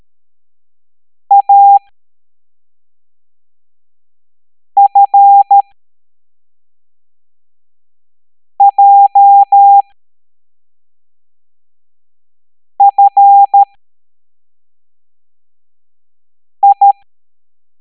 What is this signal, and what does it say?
Morse code, heard as a keyed tone, about 13 wpm, spaced as for 3 wpm, 800 Hz. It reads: AFJFI